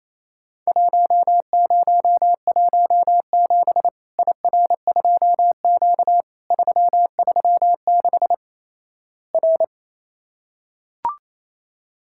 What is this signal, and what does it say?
Morse code, heard as a keyed tone, about 27 wpm, 700 Hz. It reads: 1017 IR2Q 336 R
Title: Worked All Europe DX Contest - QTC Exchange - 28wpm